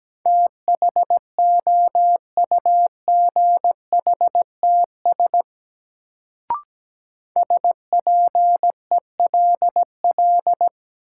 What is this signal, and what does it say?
Morse code, heard as a keyed tone, about 17 wpm, 700 Hz. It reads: THOUGHTS SPELL
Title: Sentences from Top 1000 Words - Review Part 1 - New Word Drill - 17wpm